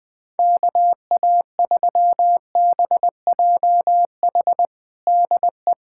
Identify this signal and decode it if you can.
Morse code, heard as a keyed tone, about 20 wpm, 700 Hz. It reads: KA3BJH DE
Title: USA Callsign DE Callsign - 6 Characters - 20wpm